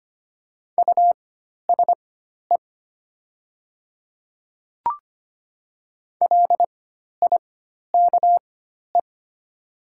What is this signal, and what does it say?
Morse code, heard as a keyed tone, about 25 wpm, spaced as for 12 wpm, 700 Hz. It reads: USE LIKE
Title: 4x Character Spacing - Top 100 Words - 25wpm